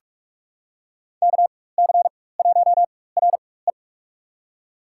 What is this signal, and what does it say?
Morse code, heard as a keyed tone, about 47 wpm, spaced as for 21 wpm, 700 Hz. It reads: KC1RE
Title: USA Call Signs - 4x Character Spacing - 45wpm